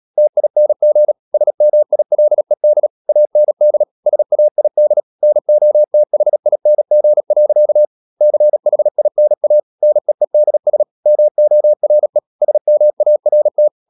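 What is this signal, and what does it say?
Morse code, heard as a keyed tone, about 37 wpm, 600 Hz. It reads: TING SMILED AND SAID NOTHING. CHINA NEEDS MORE SMART
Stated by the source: Morse code practice at 35–39 wpm sending English text of Ken Follett's novel Never, from CW med Gnister.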